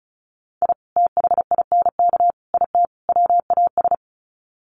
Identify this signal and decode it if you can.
Morse code, heard as a keyed tone, about 34 wpm, 700 Hz. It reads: I THINK IT WAS
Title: Sets of 4 Words 35wpm